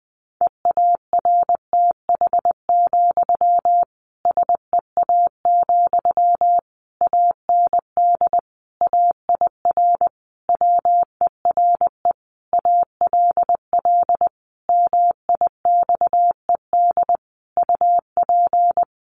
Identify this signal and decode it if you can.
Morse code, heard as a keyed tone, about 20 wpm, 700 Hz. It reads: EARTH, SEA, AND AIR WERE ALL MIXED UP
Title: Sentences from Top 1500 Words 20wpm